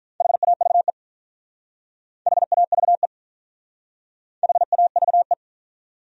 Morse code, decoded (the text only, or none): HAVE HAVE HAVE